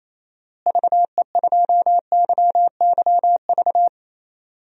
Morse code, heard as a keyed tone, about 28 wpm, 700 Hz. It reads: VE2YYV
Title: DXCC-100 Call Signs - 6 Characters - Repeated 2x - 28wpm